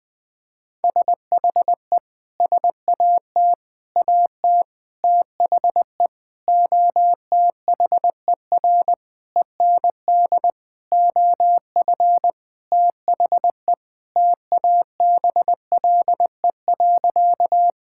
Morse code, heard as a keyed tone, about 20 wpm, 700 Hz. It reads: SHE SAT AT THE OTHER END OF THE TABLE.
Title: Sentences with 6-Word Phrases - Spoken Beginning - 20wpm